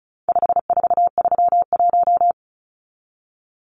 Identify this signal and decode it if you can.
Morse code, heard as a keyed tone, about 34 wpm, 700 Hz. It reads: 5431
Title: Numbers - 4-Digit 35wpm